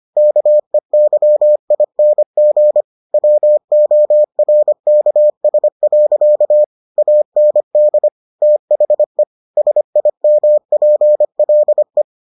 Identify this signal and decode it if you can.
Morse code, heard as a keyed tone, about 25 wpm, 600 Hz. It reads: KEYING WORKS. AND THE SIMPLE